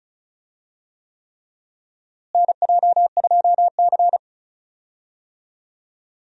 Morse code, heard as a keyed tone, about 36 wpm, 700 Hz. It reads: NJ2C
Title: USA Call Signs - 4 Characters - Repeated 3x - 35wpm